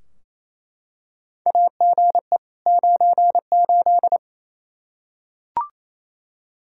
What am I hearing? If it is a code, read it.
Morse code, heard as a keyed tone, about 28 wpm, 700 Hz. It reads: AGE 98
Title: Practice QSO - Exchange 3 - More Info - 28wpm